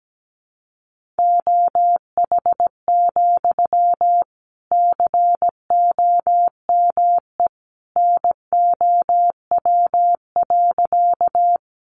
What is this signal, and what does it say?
Morse code, heard as a keyed tone, about 17 wpm, 700 Hz. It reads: OH, COME NOW.
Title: Sentences from Top 500 Words 17wpm